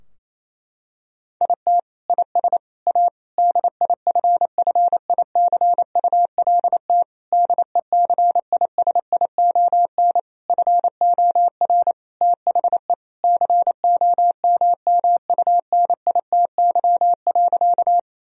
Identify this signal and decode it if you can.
Morse code, heard as a keyed tone, about 28 wpm, 700 Hz. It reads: IT IS A DIFFICULT DECISION FOR THE COMMUNITY.
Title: Sentences from Top 1000 Words - Review Part 4 - 28wpm